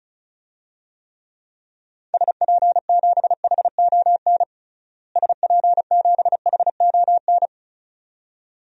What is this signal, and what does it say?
Morse code, heard as a keyed tone, about 36 wpm, 700 Hz. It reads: SP7HON SP7HON
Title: DXCC-100 Call Signs - 6 Characters - Repeated 2x - 35wpm